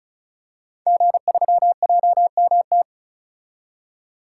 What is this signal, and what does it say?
Morse code, heard as a keyed tone, about 36 wpm, 700 Hz. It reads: G3JMT
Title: Code Groups - 5 Letter:Numbers - 35wpm